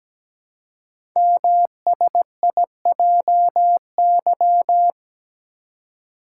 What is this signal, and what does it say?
Morse code, heard as a keyed tone, about 17 wpm, 700 Hz. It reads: MSIJY
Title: Code Groups - 5 Letters - 17wpm